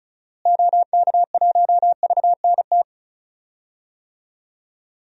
Morse code, heard as a keyed tone, about 34 wpm, 700 Hz. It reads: OK1VNT
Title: DXCC-100 Call Signs - 6 Characters - Repeated 2x - 35wpm